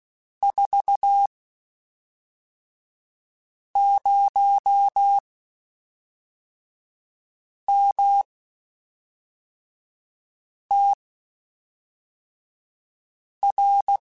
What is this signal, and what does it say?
Morse code, heard as a keyed tone, about 16 wpm, spaced as for 3 wpm, 800 Hz. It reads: 40MTR
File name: (spaced) / QSOs Spaced-mp3s / kc7sac @5 sp